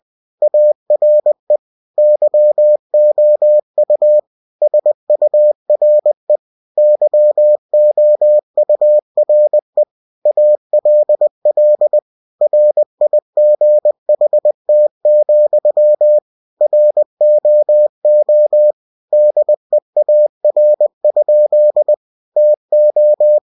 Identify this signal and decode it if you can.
Morse code, heard as a keyed tone, about 20 wpm, 600 Hz. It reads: ARE YOU SURE YOURE ALL RIGHT, ROO DEAR? TO
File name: winnie-20wpm